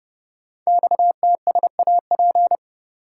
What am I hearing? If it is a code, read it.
Morse code, heard as a keyed tone, about 30 wpm, 690 Hz. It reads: XTSAP
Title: Code Groups - 5 Letters - 30wpm